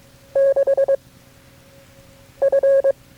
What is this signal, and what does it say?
Morse code, heard as a keyed tone, about 22 wpm, 550 Hz. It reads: 6 F
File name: FreeFiles / CW-practice / KB6MT-tape3-side1